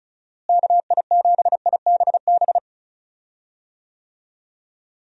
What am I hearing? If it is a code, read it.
Morse code, heard as a keyed tone, about 34 wpm, 700 Hz. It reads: KI7IBB
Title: USA Call Signs - 6 Characters - Repeated 2x - 35wpm